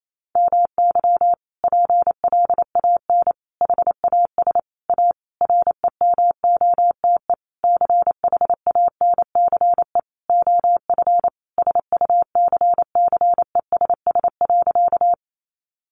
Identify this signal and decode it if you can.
Morse code, heard as a keyed tone, about 28 wpm, 700 Hz. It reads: MY PLAN HAS A REMOTE CHANCE OF SUCCESS.
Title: Sentences from Top 1300 Words 28wpm